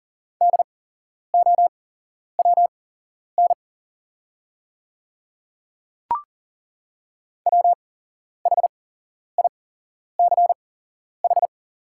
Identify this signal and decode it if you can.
Morse code, heard as a keyed tone, about 40 wpm, spaced as for 11 wpm, 690 Hz. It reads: DOWN WHICH
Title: 8x Character Spacing - Top 100 Words - 40wpm